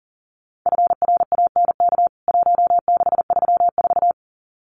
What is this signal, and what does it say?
Morse code, heard as a keyed tone, about 40 wpm, 700 Hz. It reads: FRANK 1634
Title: CWT Exchange 40wpm